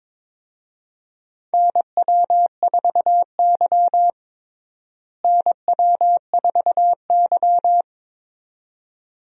Morse code, heard as a keyed tone, about 22 wpm, 700 Hz. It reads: NW4Y NW4Y
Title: USA Call Signs - 4 Characters - Repeated 2x - 22wpm